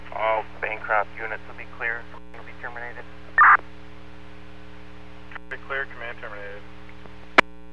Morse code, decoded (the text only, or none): none